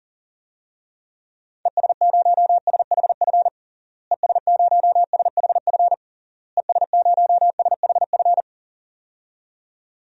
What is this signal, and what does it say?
Morse code, heard as a keyed tone, about 40 wpm, 700 Hz. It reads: ES0SHF ES0SHF ES0SHF
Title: DXCC-100 Call Signs - 6 Characters - Repeated 3x - 40wpm